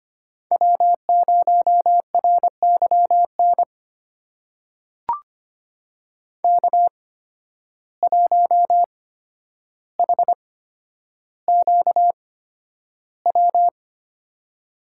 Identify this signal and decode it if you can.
Morse code, heard as a keyed tone, about 25 wpm, spaced as for 11 wpm, 700 Hz. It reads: W0RYN K1HQW
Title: USA Call Signs - 5 Characters - 8x Character Spacing - 25wpm